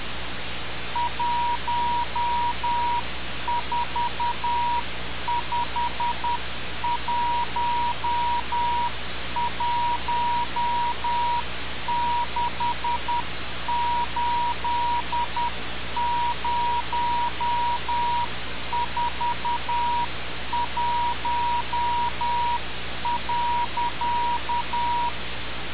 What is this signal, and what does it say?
Morse code, heard as a keyed tone, about 10 wpm, 960 Hz. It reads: 1451168041.
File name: radiocap / REC z